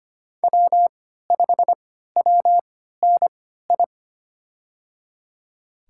Morse code, heard as a keyed tone, about 25 wpm, spaced as for 14 wpm, 700 Hz. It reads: W5WNI
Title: USA Call Signs - 5 Characters - 3x Character Spacing - 25wpm